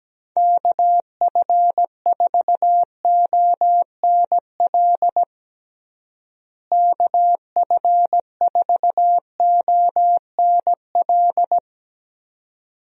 Morse code, heard as a keyed tone, about 17 wpm, 700 Hz. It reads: KF4ONL KF4ONL
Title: USA Call Signs - 6 Characters - Repeated 2x - 17wpm